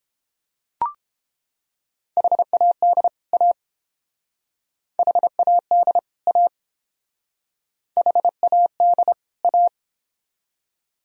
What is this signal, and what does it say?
Morse code, heard as a keyed tone, about 29 wpm, 700 Hz. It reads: HAD A HAD A HAD A